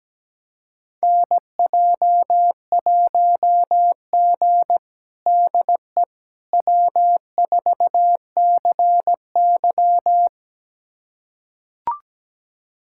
Morse code, heard as a keyed tone, about 17 wpm, 700 Hz. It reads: NJ1G DE W4CY
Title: USA Callsign DE Callsign - 4 Characters - 17wpm